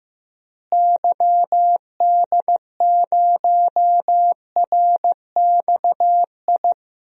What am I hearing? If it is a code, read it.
Morse code, heard as a keyed tone, about 15 wpm, 700 Hz. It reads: YD0RXI
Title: DXCC-100 Call Signs - 6 Characters - Repeated 2x - 15wpm